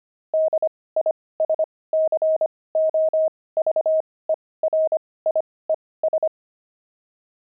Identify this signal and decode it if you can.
Morse code, heard as a keyed tone, about 25 wpm, spaced as for 18 wpm, 630 Hz. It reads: DISCOVERIES
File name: pangram-fs-18-cs-25-explorer